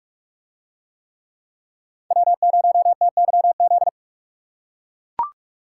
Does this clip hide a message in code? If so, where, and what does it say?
Yes: Morse code from 2.1 to 3.9 s, heard as a keyed tone, about 43 wpm, 700 Hz. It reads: W0TYZ